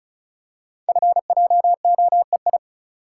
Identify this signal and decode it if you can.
Morse code, heard as a keyed tone, about 36 wpm, 690 Hz. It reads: FJOEI
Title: Code Groups - 5 Letters - 35wpm